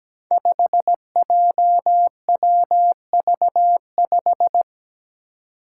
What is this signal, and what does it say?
Morse code, heard as a keyed tone, about 17 wpm, 700 Hz. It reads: 5JWV5
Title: Five Challenging Letter-Number Combinations 17wpm